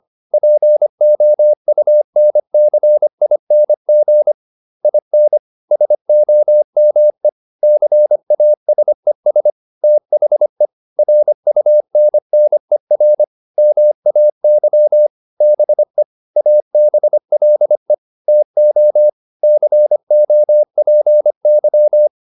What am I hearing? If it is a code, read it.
Morse code, heard as a keyed tone, about 25 wpm, 600 Hz. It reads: POUNCING IN SOME CASES THE RUNNER MAY BE ABLE TO COPY